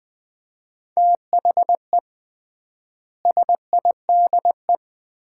Morse code, heard as a keyed tone, about 20 wpm, 700 Hz. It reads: THE SIDE
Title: 3x Word-Spacing - Sets of 5 Words - 20wpm